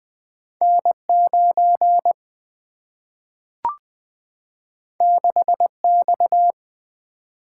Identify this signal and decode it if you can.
Morse code, heard as a keyed tone, about 20 wpm, 700 Hz. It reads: N9 6X
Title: Single-Letter Single-Number 20wpm